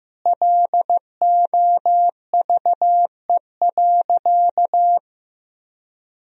Morse code, heard as a keyed tone, about 15 wpm, 700 Hz. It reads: LOVE.
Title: Sentences from Top 400 Words and Abbreviations 15wpm